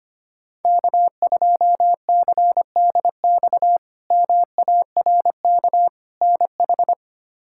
Morse code, heard as a keyed tone, about 25 wpm, 700 Hz. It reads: K2CDX MARK NH
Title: K1USN SST QSO Practice - 25wpm